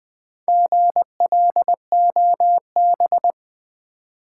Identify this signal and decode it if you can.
Morse code, heard as a keyed tone, about 20 wpm, 700 Hz. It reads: GLOB
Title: Build A Word - 1 Letter At A Time - 3 to 5-Letter Words - 20wpm